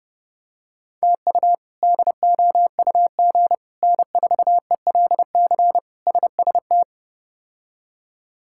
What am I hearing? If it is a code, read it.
Morse code, heard as a keyed tone, about 30 wpm, 700 Hz. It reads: TU DOUG N4ELC SST
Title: K1USN SST QSO Practice - 30wpm